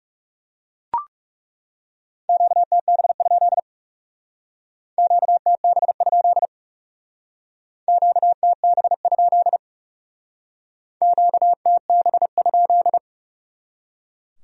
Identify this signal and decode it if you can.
Morse code, heard as a keyed tone, about 38 wpm, 700 Hz. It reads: QTB? QTB? QTB? QTB?